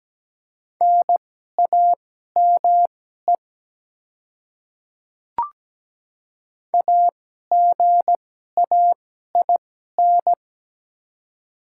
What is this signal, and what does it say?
Morse code, heard as a keyed tone, about 17 wpm, spaced as for 12 wpm, 700 Hz. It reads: NAME AGAIN
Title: 2x Character Spacing - Top 200 Words - 17wpm